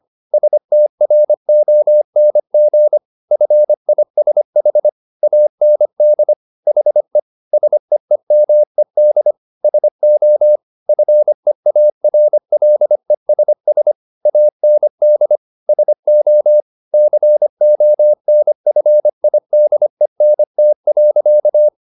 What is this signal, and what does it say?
Morse code, heard as a keyed tone, about 25 wpm, 600 Hz. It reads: STRONG FISH AND HE SEEMED SO FEARLESS AND SO CONFIDENT.